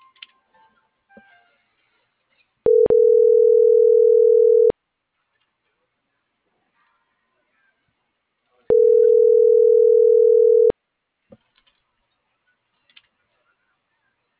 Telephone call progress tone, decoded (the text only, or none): ringback tone